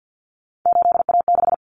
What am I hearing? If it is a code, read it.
Morse code, heard as a keyed tone, about 48 wpm, 700 Hz. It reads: 8A6
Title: Common Cut Numbers - 3-Digit 50wpm